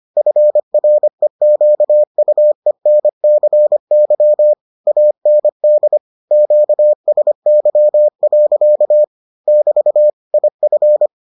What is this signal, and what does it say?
Morse code, heard as a keyed tone, about 25 wpm, 600 Hz. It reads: FREQUENCY AND QSY. <BT> IF